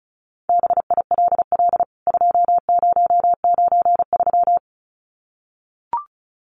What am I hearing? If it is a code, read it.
Morse code, heard as a keyed tone, about 34 wpm, 700 Hz. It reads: BILL 2093
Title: CWT Exchange 35wpm